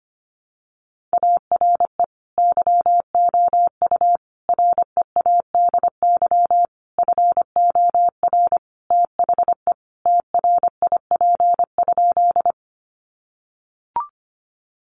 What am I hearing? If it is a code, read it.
Morse code, heard as a keyed tone, about 25 wpm, 700 Hz. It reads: ARE YOU READY FOR THE TRIP?
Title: Sentences from Top 1300 Words 25wpm